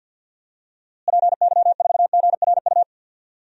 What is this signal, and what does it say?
Morse code, heard as a keyed tone, about 48 wpm, 700 Hz. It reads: PY4GRU
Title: DXCC-100 Call Signs - 6 Characters - Repeated 2x - 50wpm